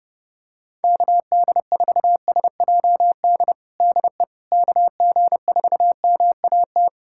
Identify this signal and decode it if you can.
Morse code, heard as a keyed tone, about 28 wpm, 700 Hz. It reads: KD4SJD DE KG4MAT
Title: USA Callsign DE Callsign 30wpm